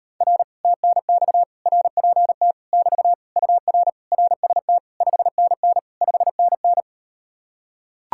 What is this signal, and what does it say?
Morse code, heard as a keyed tone, about 38 wpm, 700 Hz. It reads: R TNX RPT <BT> UR RST 5NN 5NN